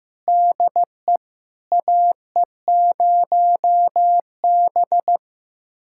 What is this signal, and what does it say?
Morse code, heard as a keyed tone, about 16 wpm, 700 Hz. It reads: DE AE0B
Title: USA Callsign DE Callsign - 4 Characters - 15wpm